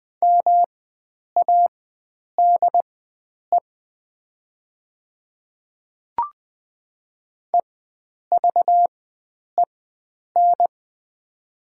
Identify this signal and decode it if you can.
Morse code, heard as a keyed tone, about 20 wpm, spaced as for 9 wpm, 700 Hz. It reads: MADE EVEN
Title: 4x Character Spacing - Top 300 Words - 20wpm